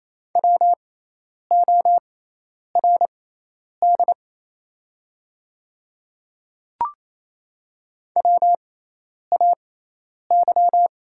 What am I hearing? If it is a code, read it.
Morse code, heard as a keyed tone, about 28 wpm, spaced as for 10 wpm, 690 Hz. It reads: WORD WAY
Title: 6x Character Spacing - Top 200 Words - 28wpm